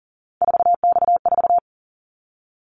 Morse code, heard as a keyed tone, about 40 wpm, 700 Hz. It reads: 4X4
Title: Callsign Prefixes for Top 50 DXCC - 40wpm